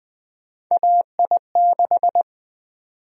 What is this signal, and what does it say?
Morse code, heard as a keyed tone, about 20 wpm, 700 Hz. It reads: AI6
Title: USA Call Sign Prefixes - 20wpm